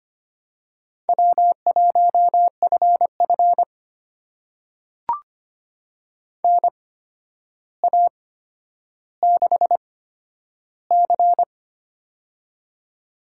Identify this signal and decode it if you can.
Morse code, heard as a keyed tone, about 25 wpm, spaced as for 11 wpm, 700 Hz. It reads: W1FF NA6C
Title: USA Call Signs - 4 Characters - 8x Character Spacing - 25wpm